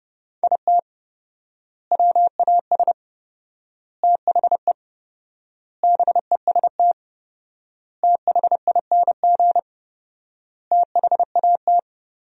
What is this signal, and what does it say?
Morse code, heard as a keyed tone, about 32 wpm, 700 Hz. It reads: IT WAS THE BEST THING THAT
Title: 4x Word-Spacing - Sentences from Top 400 Words - 30wpm